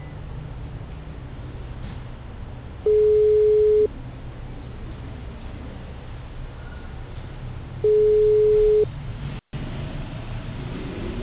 Ringback tone (Europe, 425 Hz).